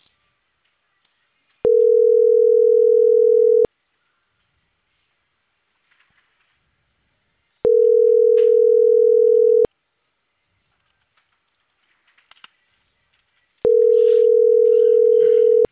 Ringback tone (North America), 440 + 480 Hz.